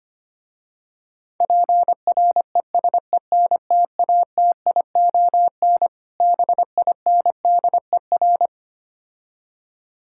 Morse code, heard as a keyed tone, about 25 wpm, 700 Hz. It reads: PRESENTATION BINDER
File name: practice / 25wpm / Difficult Words - Speed-Racing with Hints - 25wpm